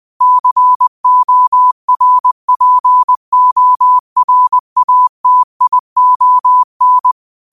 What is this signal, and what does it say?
Morse code, heard as a keyed tone, about 20 wpm, 1000 Hz. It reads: CORPORATION
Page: Quotes for Fri, 15 Aug 2025 in Morse Code at 20 words per minute.